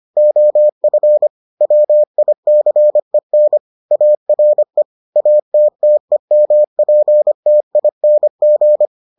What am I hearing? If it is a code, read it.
Morse code, heard as a keyed tone, about 25 wpm, 600 Hz. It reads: OF WICEN ARE ATTEMPTING